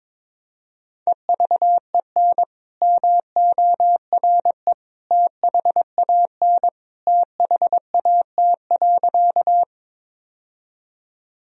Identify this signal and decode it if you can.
Morse code, heard as a keyed tone, about 22 wpm, 700 Hz. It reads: EVEN MORE THAN THAT.
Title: Sentences from Top 1000 Words - Review Part 1 - 22wpm